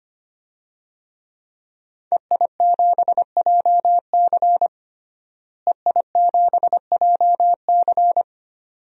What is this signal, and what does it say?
Morse code, heard as a keyed tone, about 25 wpm, 700 Hz. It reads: EI7JC EI7JC
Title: DXCC-100 Call Signs - 5 Characters - Repeated 2x - 25wpm